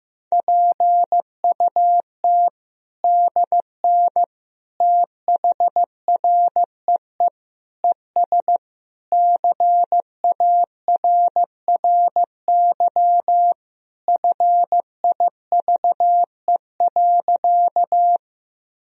Morse code, heard as a keyed tone, about 16 wpm, 700 Hz. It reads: PUT DN THREE ES CARRY FIVE.
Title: Sentences from Top 400 Words and Abbreviations 15wpm